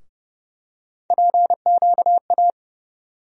Morse code, heard as a keyed tone, about 30 wpm, 700 Hz. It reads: PQA
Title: USA Call Sign Postfixes - 3 Letters 30wpm